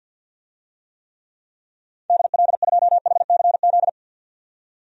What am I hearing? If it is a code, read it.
Morse code, heard as a keyed tone, about 53 wpm, 700 Hz. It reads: DL2HCZ
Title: DXCC-100 Call Signs - 6 Characters - 50wpm